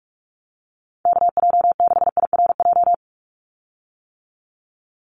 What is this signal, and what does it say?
Morse code, heard as a keyed tone, about 46 wpm, 700 Hz. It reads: KJ6IRJ
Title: USA Call Signs 45wpm